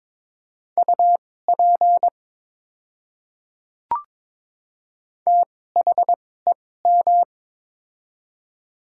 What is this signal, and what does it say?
Morse code, heard as a keyed tone, about 22 wpm, spaced as for 16 wpm, 700 Hz. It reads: UP THEM
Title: 2x Character Spacing - Top 200 Words - 22wpm